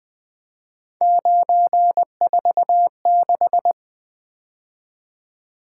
Morse code, heard as a keyed tone, about 25 wpm, 700 Hz. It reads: 946
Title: Speed-Racing - Three Letter-Number Combinations - 20wpm